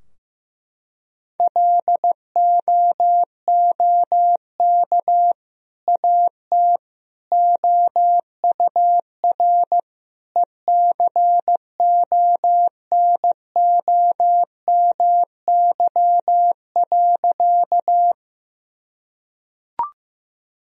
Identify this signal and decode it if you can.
Morse code, heard as a keyed tone, about 15 wpm, 700 Hz. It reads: LOOK AT OUR ECONOMY.
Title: Sentences from Top 1000 Words - Review Part 3 - 15wpm